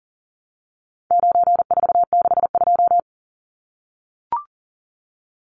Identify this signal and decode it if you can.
Morse code, heard as a keyed tone, about 40 wpm, 700 Hz. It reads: 9462
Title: Numbers - 4-Digit 40wpm